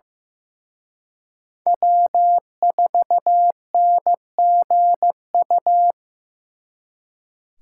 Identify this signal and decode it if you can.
Morse code, heard as a keyed tone, about 18 wpm, 700 Hz. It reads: W4NGU
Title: Speed Racing - USA 1x3 Call Signs 15wpm